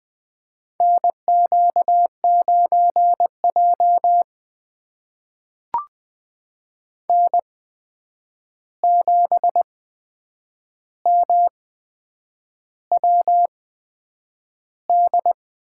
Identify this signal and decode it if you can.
Morse code, heard as a keyed tone, about 20 wpm, spaced as for 8 wpm, 700 Hz. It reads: NQ9J N7MWD
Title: USA Call Signs - 8x Character Spacing - 20wpm